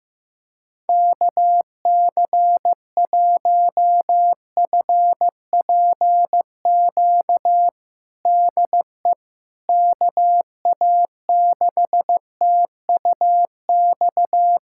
Morse code, heard as a keyed tone, about 15 wpm, 700 Hz. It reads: KC1FPQ DE KA6TUX
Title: USA Callsign DE Callsign 15wpm